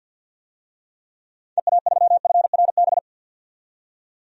Morse code, heard as a keyed tone, about 46 wpm, 700 Hz. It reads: EA3FRB
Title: DXCC-100 Call Signs - 6 Characters - 50wpm